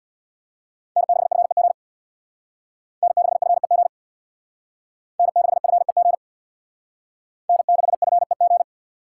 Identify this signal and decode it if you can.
Morse code, heard as a keyed tone, about 62 wpm, 700 Hz. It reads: N6FEG N6FEG N6FEG N6FEG
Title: Speed Racing - USA 1x3 Call Signs 50wpm